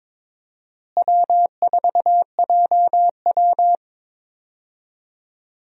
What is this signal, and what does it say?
Morse code, heard as a keyed tone, about 22 wpm, 700 Hz. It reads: W4JW
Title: USA Call Signs - 4 Characters - Repeated 3x - 22wpm